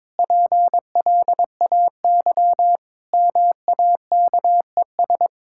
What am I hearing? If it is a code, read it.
Morse code, heard as a keyed tone, about 22 wpm, 700 Hz. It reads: PLAY MAKES
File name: Sentences with 7-Word Phrases - 22wpm